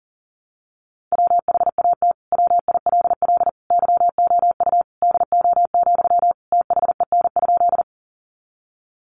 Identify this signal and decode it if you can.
Morse code, heard as a keyed tone, about 40 wpm, 700 Hz. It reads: WHAT WILL YOU DO, THEN?
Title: Sentences from Top 100 Words 40wpm